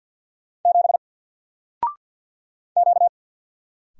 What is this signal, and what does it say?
Morse code, heard as a keyed tone, about 50 wpm, 700 Hz. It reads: 7 Q
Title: Single Letter-Number 50wpm